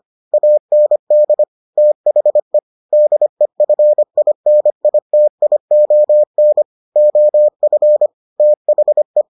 Morse code, heard as a keyed tone, about 25 wpm, 600 Hz. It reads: AND THE DEFINITION OF THE